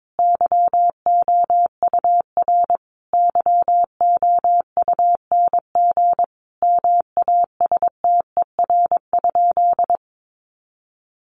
Morse code, heard as a keyed tone, about 22 wpm, 700 Hz. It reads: YOUR YOUNG MASTER?
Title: Sentences from Top 1300 Words 22wpm